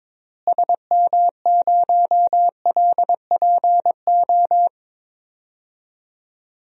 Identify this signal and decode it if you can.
Morse code, heard as a keyed tone, about 22 wpm, 700 Hz. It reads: SM0LPO
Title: DXCC-100 Call Signs - 6 Characters - Repeated 2x - 22wpm